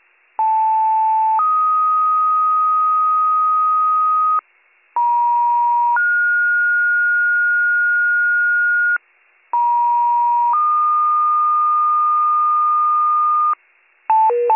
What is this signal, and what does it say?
Two-tone page, heard as a one-second tone, then a three-second tone, about 880 then 1250 Hz, about 950 then 1430 Hz, about 950 then 1220 Hz. It is heard from 0.4 to 13.5 s.